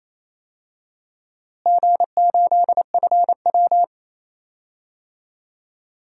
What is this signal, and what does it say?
Morse code, heard as a keyed tone, about 28 wpm, 700 Hz. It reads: G8FW
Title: DXCC-100 Call Signs - 4 Characters - Repeated 3x - 28wpm